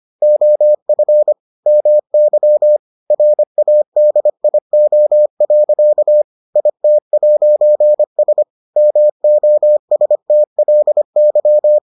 Morse code, heard as a keyed tone, about 25 wpm, 600 Hz. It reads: OF MY RADIO. IT'S MOSTLY